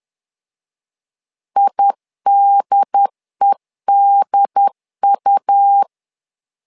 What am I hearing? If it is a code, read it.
Morse code, heard as a keyed tone, about 10 wpm, 800 Hz. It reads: IDEDU
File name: LEZIONI / 2 UHDN / 2 UHD 10wpm 1826